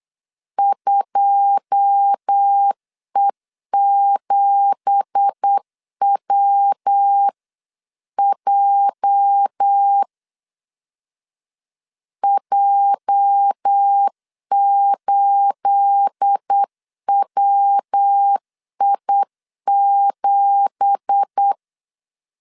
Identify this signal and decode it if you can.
Morse code, heard as a keyed tone, about 8 wpm, 800 Hz. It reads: 2E7WJ J8WI7
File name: 8wpm